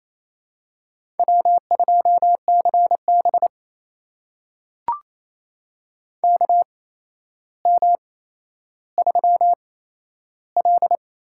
Morse code, heard as a keyed tone, about 28 wpm, spaced as for 12 wpm, 700 Hz. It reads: W2CB KM3L
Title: USA Call Signs - 8x Character Spacing - 28wpm